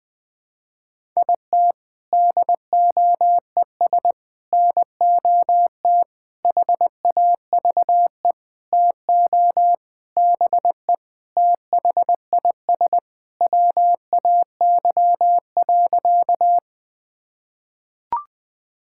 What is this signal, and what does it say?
Morse code, heard as a keyed tone, about 20 wpm, 700 Hz. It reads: IT DOES NOT HAVE TO BE THIS WAY.
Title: Sentences with 5-Word Phrases - 20wpm